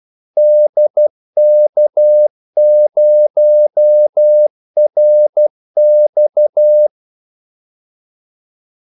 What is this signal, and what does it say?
Morse code, heard as a keyed tone, about 12 wpm, 600 Hz. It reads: DK0RX
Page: 2 ekstra ordmellemrum mellem kaldesignalerne